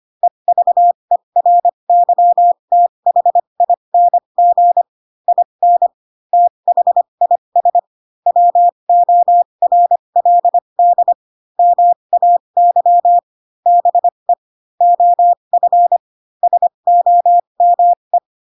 Morse code, heard as a keyed tone, about 25 wpm, spaced as for 22 wpm, 700 Hz. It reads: EVERYTHING IN THIS WORLD MAY BE OF SOME
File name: Pinocchio / Pinocchio ch7 18 25 ik1hge